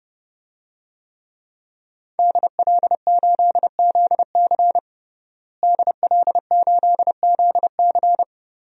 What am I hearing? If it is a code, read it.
Morse code, heard as a keyed tone, about 30 wpm, 700 Hz. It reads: DL8ZC DL8ZC
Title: DXCC-100 Call Signs - 5 Characters - Repeated 3x - 30wpm